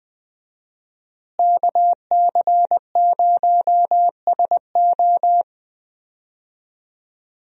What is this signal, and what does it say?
Morse code, heard as a keyed tone, about 20 wpm, 700 Hz. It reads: KC0SO
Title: USA Call Signs - 5 Characters - Repeated 2x - 20wpm